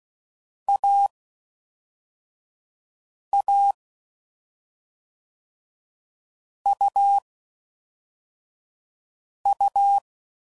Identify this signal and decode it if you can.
Morse code, heard as a keyed tone, about 16 wpm, 800 Hz. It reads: A A U U
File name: Combinations / Difficult Combos / AUV AUV4